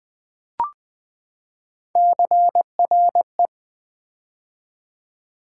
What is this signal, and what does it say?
Morse code, heard as a keyed tone, about 20 wpm, 700 Hz. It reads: CRE
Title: Build A Word - 3 Letters At A Time - 8-Letter Words - 20wpm